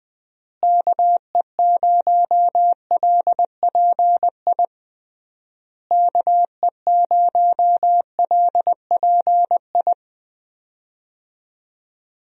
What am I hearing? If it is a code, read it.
Morse code, heard as a keyed tone, about 20 wpm, 700 Hz. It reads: KE0LPI KE0LPI
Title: USA Call Signs - 6 Characters - Repeated 3x - 20wpm